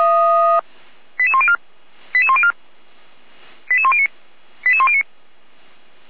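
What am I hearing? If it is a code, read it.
Five-tone selective call (ZVEI), heard as quick runs of five tones, tones of about 70 ms. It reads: 88104 88104 88109 88109
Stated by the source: ZVEI five-tone selective call 88190, received on an SDR.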